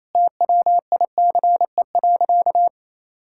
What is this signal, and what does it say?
Morse code, heard as a keyed tone, about 28 wpm, 700 Hz. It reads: TWICE.
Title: Sentences from Top 1600 Words 28wpm